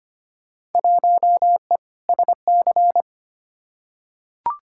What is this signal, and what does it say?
Morse code, heard as a keyed tone, about 25 wpm, 700 Hz. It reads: 1E SC
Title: Field Day Exchanges - 25wpm